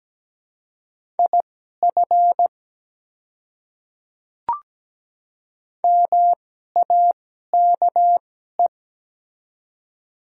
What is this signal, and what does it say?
Morse code, heard as a keyed tone, about 17 wpm, spaced as for 12 wpm, 700 Hz. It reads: IF MAKE
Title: 2x Character Spacing - Top 100 Words - 17wpm